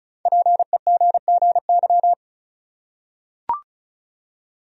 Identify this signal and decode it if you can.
Morse code, heard as a keyed tone, about 36 wpm, 700 Hz. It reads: PEGGY
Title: USA Common Names - Max 5 Letters - 35wpm